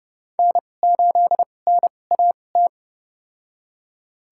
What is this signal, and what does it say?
Morse code, heard as a keyed tone, about 30 wpm, spaced as for 22 wpm, 700 Hz. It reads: N8NAT
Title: USA Call Signs - 5 Characters - 2x Character Spacing - 30wpm